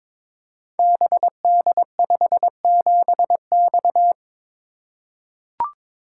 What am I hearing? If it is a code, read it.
Morse code, heard as a keyed tone, about 22 wpm, 700 Hz. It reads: BD57X
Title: Code Groups - 5 Letter:Numbers - 22wpm